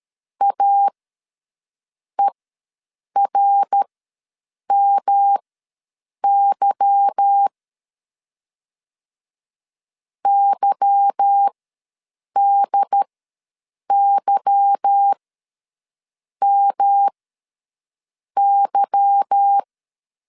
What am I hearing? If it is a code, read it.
Morse code, heard as a keyed tone, about 13 wpm, spaced as for 6 wpm, 800 Hz. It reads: AERMY YDYMY
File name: LEZIONI / 8 YJ45 / 8 Y ws 12wpm 3409